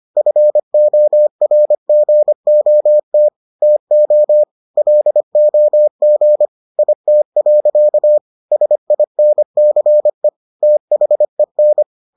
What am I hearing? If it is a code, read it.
Morse code, heard as a keyed tone, about 25 wpm, 600 Hz. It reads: FORGOT TO LOG IT. SINCE THEN